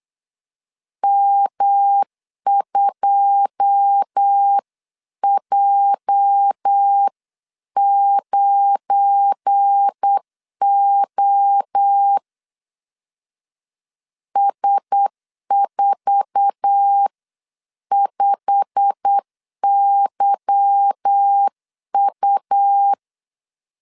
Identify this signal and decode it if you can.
Morse code, heard as a keyed tone, about 8 wpm, 800 Hz. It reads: M2J9O S45YU
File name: CW / DET / 1 9 1500C S C 8wpm 5653